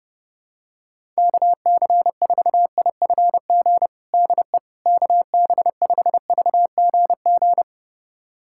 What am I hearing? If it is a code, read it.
Morse code, heard as a keyed tone, about 32 wpm, 700 Hz. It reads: KC4IFG DE KB5VGG
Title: USA Callsign DE Callsign - 6 Characters - 30wpm